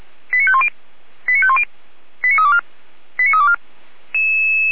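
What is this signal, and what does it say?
Five-tone selective call (ZVEI), heard as quick runs of five tones, tones of about 70 ms. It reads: 89510 89510 89324 89324
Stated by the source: ZVEI five-tone selective call received on SDR.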